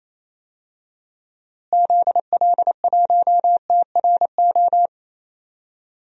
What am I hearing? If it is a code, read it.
Morse code, heard as a keyed tone, about 28 wpm, 700 Hz. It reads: ZL1TRO
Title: DXCC-100 Call Signs - 6 Characters - Repeated 2x - 28wpm